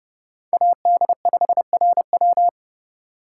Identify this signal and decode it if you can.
Morse code, heard as a keyed tone, about 32 wpm, 700 Hz. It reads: AD5RW
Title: USA Call Signs - 3x Character Spacing - 30wpm